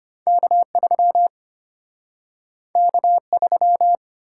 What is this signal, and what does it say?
Morse code, heard as a keyed tone, about 34 wpm, 700 Hz. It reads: K3 K3